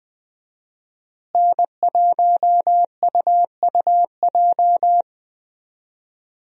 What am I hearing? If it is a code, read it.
Morse code, heard as a keyed tone, about 20 wpm, 700 Hz. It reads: N1UUJ
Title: USA Call Signs - 5 Characters - Repeated 3x - 20wpm